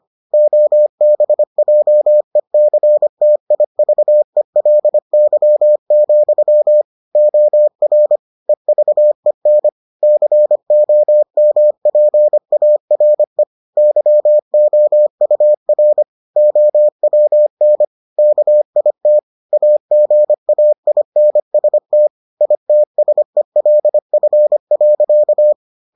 Morse code, heard as a keyed tone, about 25 wpm, 600 Hz. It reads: OBJECTIVELY, OR EVEN COMPARE YOUR OWN KIT AGAINST ITSELF.